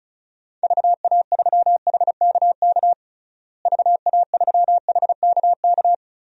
Morse code, heard as a keyed tone, about 34 wpm, 700 Hz. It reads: VA3HKK VA3HKK
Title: DXCC-100 Call Signs - 6 Characters - Repeated 2x - 35wpm